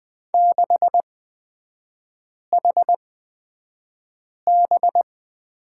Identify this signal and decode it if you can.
Morse code, heard as a keyed tone, about 20 wpm, spaced as for 5 wpm, 700 Hz. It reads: 6HB
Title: 250 Lesson 40e - BK - Three Characters 20wpm